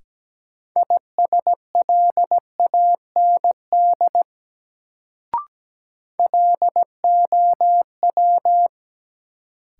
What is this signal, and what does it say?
Morse code, heard as a keyed tone, about 17 wpm, 700 Hz. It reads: ISLAND LOW
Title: Sentences from Top 500 Words - New Word Drill - 17wpm